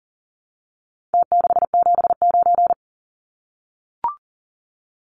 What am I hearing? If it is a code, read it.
Morse code, heard as a keyed tone, about 40 wpm, 700 Hz. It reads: T679
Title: Numbers - Contest Style - 4-Digit 40wpm